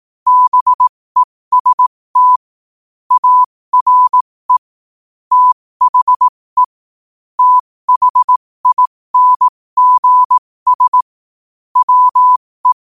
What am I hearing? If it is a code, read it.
Morse code, heard as a keyed tone, about 18 wpm, spaced as for 15 wpm, 1000 Hz. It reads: BEST ARE THE THINGS WE
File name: qotdfiles / 15wpm